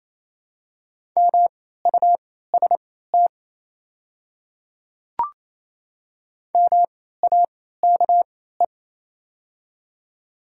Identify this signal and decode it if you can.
Morse code, heard as a keyed tone, about 28 wpm, spaced as for 16 wpm, 700 Hz. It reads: MUST MAKE
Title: 3x Character Spacing - Top 200 Words - 28wpm